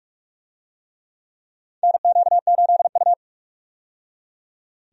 Morse code, heard as a keyed tone, about 48 wpm, 700 Hz. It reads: NQ8U
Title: USA Call Signs - 4 Characters - 45wpm